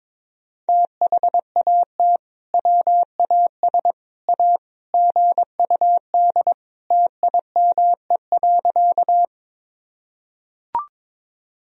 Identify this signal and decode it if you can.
Morse code, heard as a keyed tone, about 22 wpm, 700 Hz. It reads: THAT WAS A GUD TIME.
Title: Sentences from Top 200 Words and Abbreviations 22wpm